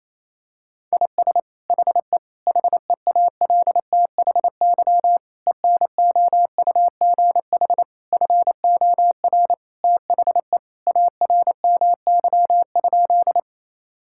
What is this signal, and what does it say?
Morse code, heard as a keyed tone, about 28 wpm, 700 Hz. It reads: IS HE HEALTHY ENOUGH FOR THE ARMY?
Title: Sentences from Top 1000 Words - Review Part 5 - 28wpm